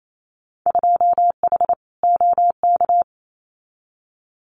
Morse code, heard as a keyed tone, about 28 wpm, 700 Hz. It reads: 2H OK
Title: Winter Field Day Exchanges Encore 28wpm